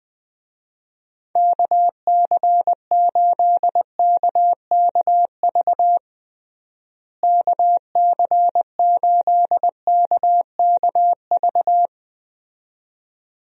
Morse code, heard as a keyed tone, about 20 wpm, 700 Hz. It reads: KC8KKV KC8KKV
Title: USA Call Signs - 6 Characters - Repeated 2x - 20wpm